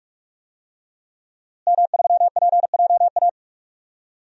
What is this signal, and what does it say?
Morse code, heard as a keyed tone, about 45 wpm, 700 Hz. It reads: M3PJA
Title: DXCC-100 Call Signs - 5 Characters - Repeated 2x - 45wpm